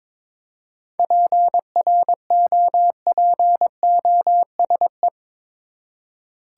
Morse code, heard as a keyed tone, about 22 wpm, 700 Hz. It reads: PROPOSE
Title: Seven Letter Words - 22wpm